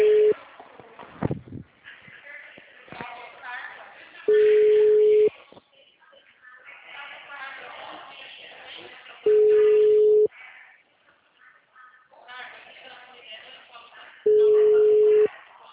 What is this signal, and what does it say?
Ringback tone (Europe, 425 Hz), with speech.